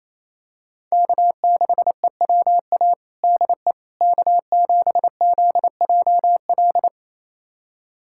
Morse code, heard as a keyed tone, about 28 wpm, 700 Hz. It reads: K6EWA DE K7ZJL
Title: USA Callsign DE Callsign - 5 Characters - 28wpm